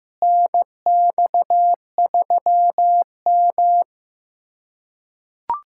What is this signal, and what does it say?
Morse code, heard as a keyed tone, about 16 wpm, 700 Hz. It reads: NX3M
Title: USA Callsign DE Callsign - 4 Characters - 15wpm